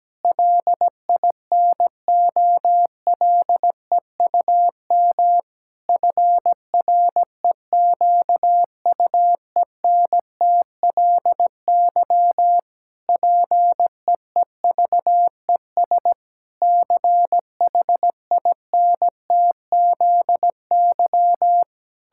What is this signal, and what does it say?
Morse code, heard as a keyed tone, about 17 wpm, 700 Hz. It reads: LINOLEUM FREQUENTLY PEEVES CHINTZY
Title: Pangram Challenge v2 - 17wpm